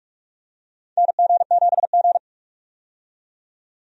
Morse code, heard as a keyed tone, about 46 wpm, 700 Hz. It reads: NG7G
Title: USA Call Signs - 4 Characters - 45wpm